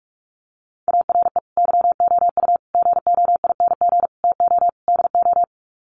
Morse code, heard as a keyed tone, about 44 wpm, 700 Hz. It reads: ARE YOU GOING TO DO
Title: Sets of 5 Words 45wpm